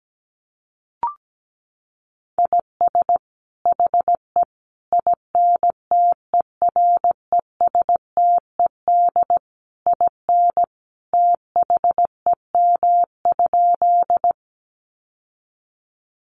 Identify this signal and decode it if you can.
Morse code, heard as a keyed tone, about 17 wpm, 700 Hz. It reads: IS HE INTERESTED IN THEM?
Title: Sentences from Top 900 Words 17wpm